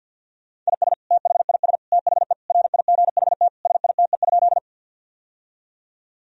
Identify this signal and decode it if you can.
Morse code, heard as a keyed tone, about 50 wpm, 700 Hz. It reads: IS THIS THE RIGHT SITE?
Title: Sentences from Top 1000 Words - Review Part 4 - 50wpm